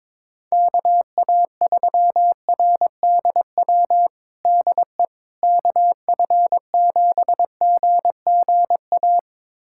Morse code, heard as a keyed tone, about 22 wpm, 700 Hz. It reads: KA3RDW DE KF7GGA
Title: USA Callsign DE Callsign 22wpm